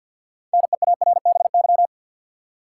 Morse code, heard as a keyed tone, about 47 wpm, 700 Hz. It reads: NEARBY
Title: Six Letter Words - 50wpm